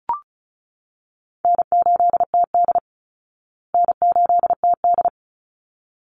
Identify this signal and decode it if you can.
Morse code, heard as a keyed tone, about 34 wpm, 700 Hz. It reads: N8TD N8TD
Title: Easy Call Signs Repeated 2x - 35wpm